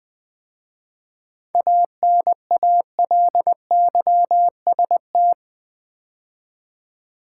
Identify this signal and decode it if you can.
Morse code, heard as a keyed tone, about 20 wpm, 700 Hz. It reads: ANALYST
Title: 7 Letter Words - First 3 Letters Given - 20wpm